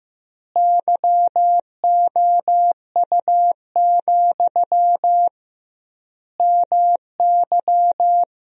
Morse code, heard as a keyed tone, about 15 wpm, 700 Hz. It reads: YOU, MY
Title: 2x Word-Spacing - Sentences from Top 400 Words - 15wpm